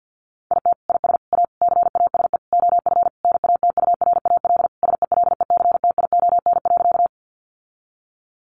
Morse code, heard as a keyed tone, about 50 wpm, 700 Hz. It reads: IT IS A CASE OF NATURAL SELECTION.
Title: Sentences from Top 1400 Words 50wpm